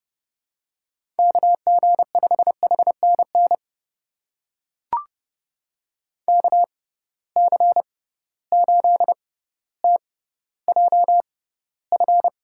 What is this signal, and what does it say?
Morse code, heard as a keyed tone, about 29 wpm, spaced as for 15 wpm, 700 Hz. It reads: KG5HNN KC8TJF
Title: USA Call Signs - 6 Characters - 6x Character Spacing - 30wpm